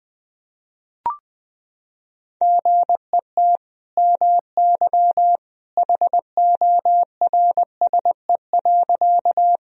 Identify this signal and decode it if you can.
Morse code, heard as a keyed tone, about 20 wpm, 700 Hz. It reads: GET MY HORSE.
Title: Sentences from Top 1000 Words - Review Part 1 - 20wpm